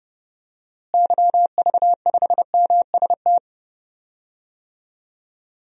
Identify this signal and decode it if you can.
Morse code, heard as a keyed tone, about 30 wpm, 700 Hz. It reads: YV5MST
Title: DXCC-100 Call Signs - 6 Characters - Repeated 2x - 30wpm